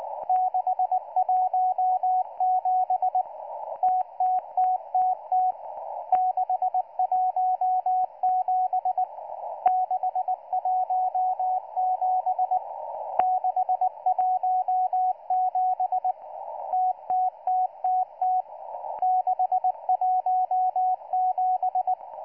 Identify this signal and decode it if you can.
Morse code, heard as a keyed tone, about 19 wpm, 740 Hz, with radio noise. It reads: 617 TTTTT 617 617 617 TTTTT 617